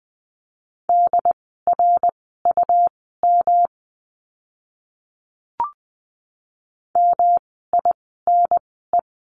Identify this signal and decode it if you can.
Morse code, heard as a keyed tone, about 20 wpm, spaced as for 15 wpm, 700 Hz. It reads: DRUM MINE
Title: 108 Lesson 18f - Letter m - Words 20wpm